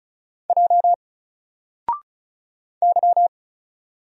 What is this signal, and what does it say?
Morse code, heard as a keyed tone, about 35 wpm, 700 Hz. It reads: J Y
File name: Single-Letter-Number-35wpm